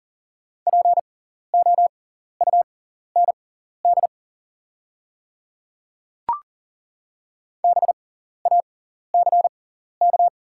Morse code, heard as a keyed tone, about 39 wpm, spaced as for 14 wpm, 690 Hz. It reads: POUND BACK
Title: 6x Character Spacing - Top 500 Words - 40wpm